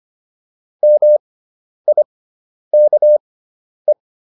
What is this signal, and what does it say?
Morse code, heard as a keyed tone, about 25 wpm, spaced as for 10 wpm, 600 Hz. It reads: MIKE
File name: QSO310 10